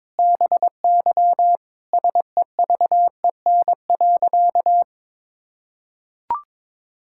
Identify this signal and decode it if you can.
Morse code, heard as a keyed tone, about 22 wpm, 700 Hz. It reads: BY SEVEN.
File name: practice / 22wpm / Sentences with 4-Word Phrases - 22wpm